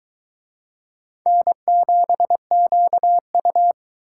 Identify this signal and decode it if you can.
Morse code, heard as a keyed tone, about 22 wpm, 700 Hz. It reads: N7QU